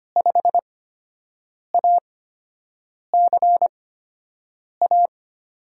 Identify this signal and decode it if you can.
Morse code, heard as a keyed tone, about 25 wpm, spaced as for 7 wpm, 700 Hz. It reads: 5ACA
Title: USA Call Signs - 8x Character Spacing - 25wpm